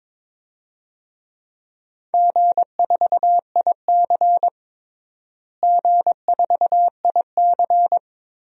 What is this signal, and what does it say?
Morse code, heard as a keyed tone, about 22 wpm, 700 Hz. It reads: G4IC G4IC
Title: DXCC-100 Call Signs - 4 Characters - Repeated 3x - 22wpm